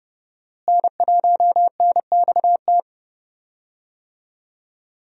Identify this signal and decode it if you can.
Morse code, heard as a keyed tone, about 30 wpm, 700 Hz. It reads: N1NXT
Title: USA Call Signs - 5 Characters - Repeated 2x - 30wpm